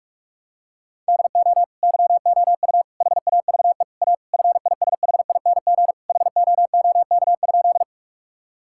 Morse code, heard as a keyed tone, about 41 wpm, 700 Hz. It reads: DO YOU HAVE A FISHING HOOK?
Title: Sentences from Top 1600 Words 45wpm